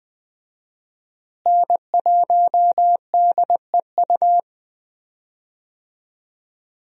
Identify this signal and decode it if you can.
Morse code, heard as a keyed tone, about 20 wpm, 700 Hz. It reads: N1DEU
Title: USA Call Signs - 5 Characters - Repeated 2x - 20wpm